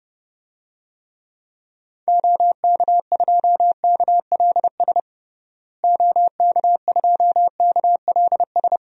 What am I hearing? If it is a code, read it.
Morse code, heard as a keyed tone, about 30 wpm, 700 Hz. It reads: OK2KLS OK2KLS
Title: DXCC-100 Call Signs - 6 Characters - Repeated 3x - 30wpm